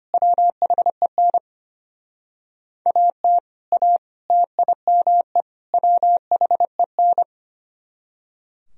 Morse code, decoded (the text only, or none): WHEN AT A TIME WHEN